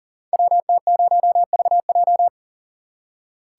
Morse code, heard as a keyed tone, about 42 wpm, 690 Hz. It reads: WT0VJ
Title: Code Groups - 5 Letter:Numbers - 40wpm